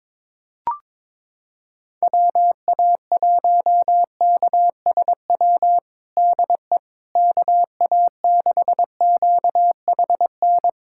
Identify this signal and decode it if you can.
Morse code, heard as a keyed tone, about 21 wpm, 700 Hz. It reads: WA1KSW DE KA6QHN
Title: USA Callsign DE Callsign - 6 Characters - 22wpm